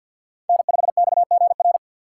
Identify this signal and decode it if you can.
Morse code, heard as a keyed tone, about 45 wpm, 700 Hz. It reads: N5XGR
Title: USA Call Signs - 3x Character Spacing - 50wpm